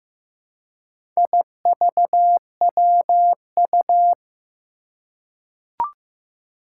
Morse code, heard as a keyed tone, about 15 wpm, 700 Hz. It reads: IVWU
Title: Four Challenging Letter-Number Combinations 15wpm